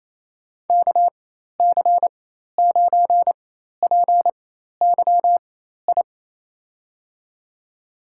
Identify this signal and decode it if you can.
Morse code, heard as a keyed tone, about 28 wpm, spaced as for 13 wpm, 700 Hz. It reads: KC9PYI
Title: USA Call Signs - 6 Characters - 4x Character Spacing - 28wpm